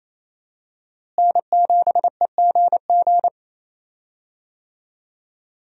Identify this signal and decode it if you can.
Morse code, heard as a keyed tone, about 27 wpm, 700 Hz. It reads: N7EGG
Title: USA Call Signs - 5 Characters - Repeated 3x - 28wpm